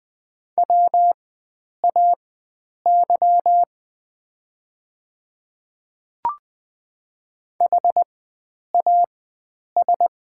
Morse code, heard as a keyed tone, about 20 wpm, spaced as for 9 wpm, 700 Hz. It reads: WAY HAS
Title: 4x Character Spacing - Top 100 Words - 20wpm